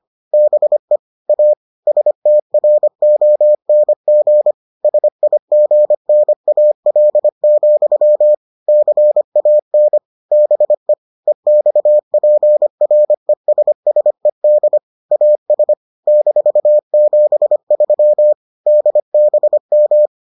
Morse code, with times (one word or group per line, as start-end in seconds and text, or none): 0.3-1.0 BE
1.3-1.5 A
1.9-4.5 STRONG
4.8-8.3 SIGNAL,
8.7-10.0 CAN
10.3-10.9 BE
11.3-14.8 EXPRESSED
15.1-15.7 AS
16.1-18.3 -73
18.7-20.1 DBM